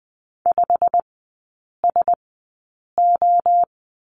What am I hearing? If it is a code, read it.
Morse code, heard as a keyed tone, about 20 wpm, spaced as for 8 wpm, 700 Hz. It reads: 5SO
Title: 114 Lesson 19e - Letter w - Three Characters 20wpm